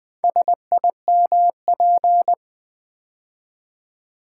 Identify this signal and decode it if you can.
Morse code, heard as a keyed tone, about 20 wpm, 700 Hz. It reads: SIMP
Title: Build A Word - 2 Letters At A Time - 6-Letter Words - 20wpm